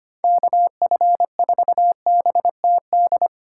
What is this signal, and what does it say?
Morse code, heard as a keyed tone, about 25 wpm, 700 Hz. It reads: KF4BTD
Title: USA Call Signs - 6 Characters - 4x Character Spacing - 25wpm